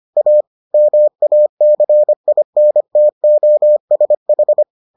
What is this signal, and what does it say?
Morse code, heard as a keyed tone, about 25 wpm, 600 Hz. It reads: A MACINTOSH